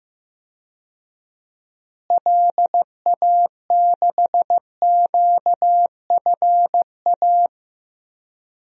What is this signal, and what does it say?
Morse code, heard as a keyed tone, about 15 wpm, 700 Hz. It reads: LA6QFA
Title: DXCC-100 Call Signs - 6 Characters - Repeated 2x - 15wpm